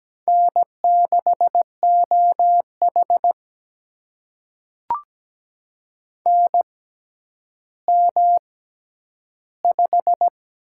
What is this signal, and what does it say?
Morse code, heard as a keyed tone, about 17 wpm, spaced as for 10 wpm, 700 Hz. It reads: N6OH NM5
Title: USA Call Signs - 4 Characters - 6x Character Spacing - 17wpm